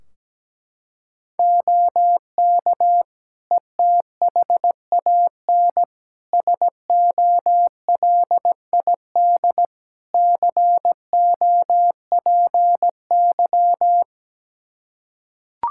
Morse code, heard as a keyed tone, about 17 wpm, 700 Hz. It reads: OK ETHAN SOLID COPY
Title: Practice QSO - Exchange 3 - More Info - 17wpm